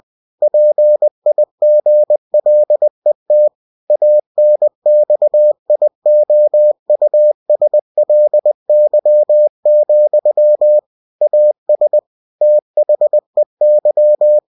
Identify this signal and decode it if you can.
Morse code, heard as a keyed tone, about 20 wpm, 600 Hz. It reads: PIGLET ANXIOUSLY, AS THEY
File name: cw-books / winnie-the-pooh / winnie-20wpm-page-0043